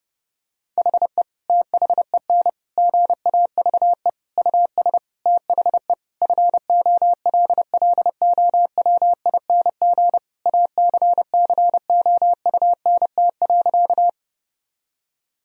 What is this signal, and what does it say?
Morse code, heard as a keyed tone, about 28 wpm, 700 Hz. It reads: HE THEN GAVE US THE FOLLOWING ACCOUNT.
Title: Sentences from Top 1000 Words - Review Part 4 - 30wpm